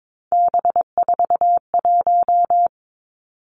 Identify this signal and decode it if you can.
Morse code, heard as a keyed tone, about 22 wpm, 700 Hz. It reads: B41
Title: Three Challenging Letter-Number Combinations 22wpm